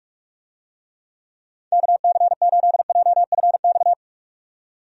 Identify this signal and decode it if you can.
Morse code, heard as a keyed tone, about 41 wpm, 700 Hz. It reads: KC8JFX
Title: USA Call Signs - 6 Characters - 45wpm